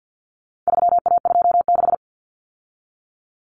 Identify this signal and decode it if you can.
Morse code, heard as a keyed tone, about 50 wpm, 700 Hz. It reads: 3A26
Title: Common Cut Numbers - 4-Digit 50wpm